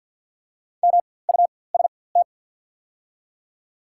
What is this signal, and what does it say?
Morse code, heard as a keyed tone, about 48 wpm, spaced as for 23 wpm, 700 Hz. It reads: MUST
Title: 4x Character Spacing - Top 300 Words - 50wpm